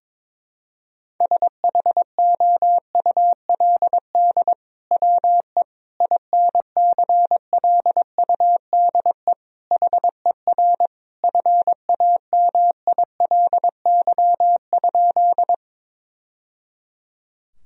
Morse code, heard as a keyed tone, about 21 wpm, 700 Hz. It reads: SHOULD WE INCLUDE HER FAMILY?
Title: Sentences from Top 1000 Words - Review Part 3 - 22wpm